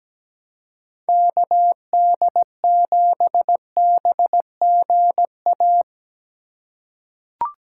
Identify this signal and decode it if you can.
Morse code, heard as a keyed tone, about 17 wpm, 700 Hz. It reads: KD7BGA
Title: USA Call Signs - 2x Character Spacing - 17wpm